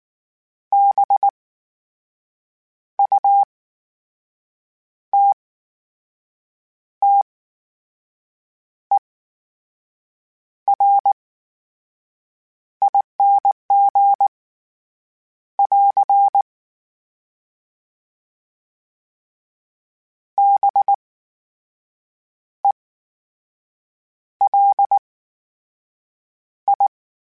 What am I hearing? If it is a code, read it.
Morse code, heard as a keyed tone, about 19 wpm, spaced as for 5 wpm, 800 Hz. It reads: BUTTERING <AR> BELI
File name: Special INGs / Special INGs-mp3s / VerySlowToFaster Ing @18